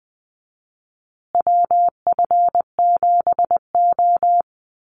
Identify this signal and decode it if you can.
Morse code, heard as a keyed tone, about 20 wpm, 700 Hz. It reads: WF7O
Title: USA Call Signs Repeated 2x 20wpm